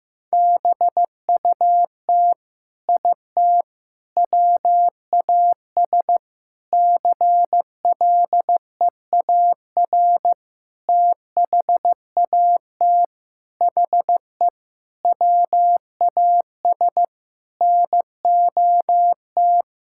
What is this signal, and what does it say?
Morse code, heard as a keyed tone, about 16 wpm, 700 Hz. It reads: BUT IT WAS CLEAR THAT HE WAS NOT
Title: Sentences from Top 1600 Words 15wpm